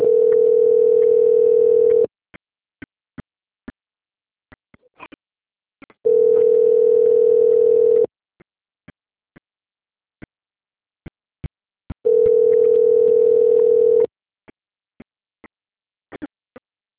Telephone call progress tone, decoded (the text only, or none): ringback tone